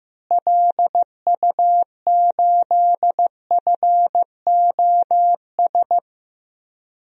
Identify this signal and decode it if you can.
Morse code, heard as a keyed tone, about 15 wpm, 700 Hz. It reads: LU8FOS
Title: DXCC-100 Call Signs - 6 Characters - Repeated 3x - 15wpm